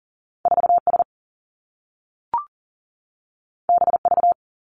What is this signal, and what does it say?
Morse code, heard as a keyed tone, about 42 wpm, 700 Hz. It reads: 4 S B V
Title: Two Challenging Letter-Number Combinations 40wpm